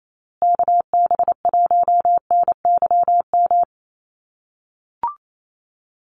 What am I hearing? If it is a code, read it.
Morse code, heard as a keyed tone, about 28 wpm, 700 Hz. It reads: KB1NYM
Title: USA Call Signs Repeated 2x 28wpm